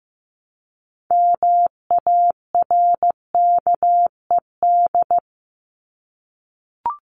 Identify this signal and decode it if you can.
Morse code, heard as a keyed tone, about 15 wpm, 700 Hz. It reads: MARKED
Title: ED Suffix - Top 100 Words with Base Words 15wpm